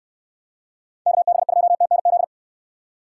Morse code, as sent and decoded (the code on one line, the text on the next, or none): -.- -... ..--- - -. --..
KB2TNZ